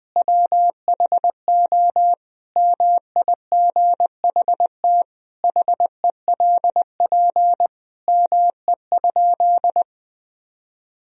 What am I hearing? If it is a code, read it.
Morse code, heard as a keyed tone, about 20 wpm, 700 Hz. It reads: WHO MIGHT HELP ME?
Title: Sentences from Top 1000 Words - Review Part 3 - 20wpm